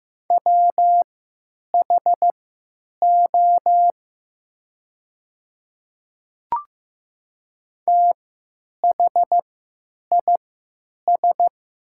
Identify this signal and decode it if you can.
Morse code, heard as a keyed tone, about 15 wpm, spaced as for 9 wpm, 700 Hz. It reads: WHO THIS
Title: 3x Character Spacing - Top 200 Words - 15wpm